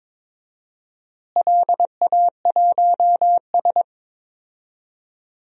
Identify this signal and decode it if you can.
Morse code, heard as a keyed tone, about 22 wpm, 700 Hz. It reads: LA1S
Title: DXCC-100 Call Signs - 4 Characters - Repeated 2x - 22wpm